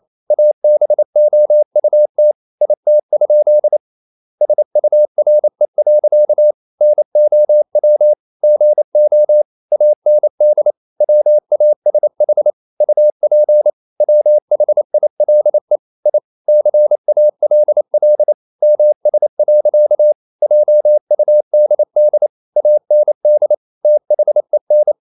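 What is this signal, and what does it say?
Morse code, heard as a keyed tone, about 28 wpm, 600 Hz. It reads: ABOUT IT? SURE. NOW GO AND WASH UP WHILE I CALL MS. JUDD AND THEN
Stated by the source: Morse code practice at 28 wpm, text of Ken Follett's novel Never in English, from CW med Gnister.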